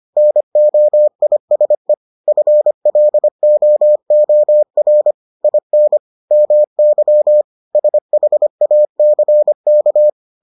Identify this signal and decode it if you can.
Morse code, heard as a keyed tone, about 25 wpm, 600 Hz. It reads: NOISE FLOOR IN MY SHACK